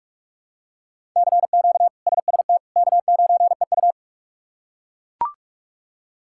Morse code, heard as a keyed tone, about 41 wpm, 700 Hz. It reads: CQ SST K9EU
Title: K1USN SST QSO Practice - 45wpm